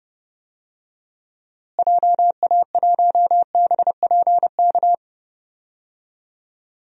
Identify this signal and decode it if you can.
Morse code, heard as a keyed tone, about 32 wpm, 700 Hz. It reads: JA1BPK
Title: DXCC-100 Call Signs - 6 Characters - Repeated 3x - 30wpm